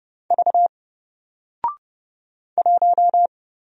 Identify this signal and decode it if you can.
Morse code, heard as a keyed tone, about 30 wpm, 700 Hz. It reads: V 1
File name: Single-Letter-Number-30wpm